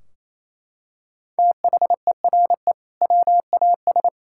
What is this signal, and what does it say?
Morse code, heard as a keyed tone, about 28 wpm, 700 Hz. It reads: THERE WAS
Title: Sentences from Top 1600 Words 28wpm